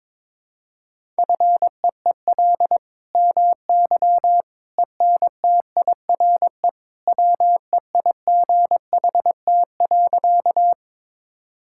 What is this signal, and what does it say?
Morse code, heard as a keyed tone, about 21 wpm, 700 Hz. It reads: FEEL MY ENTIRE WEIGHT.
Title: Sentences from Top 1000 Words - Review Part 5 - 22wpm